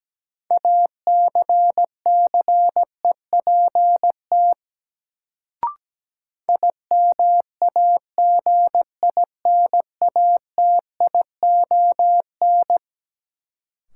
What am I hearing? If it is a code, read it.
Morse code, heard as a keyed tone, about 17 wpm, 700 Hz. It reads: ACCEPT IMAGINATION
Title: Sentences from Top 1300 Words - New Word Drill - 17wpm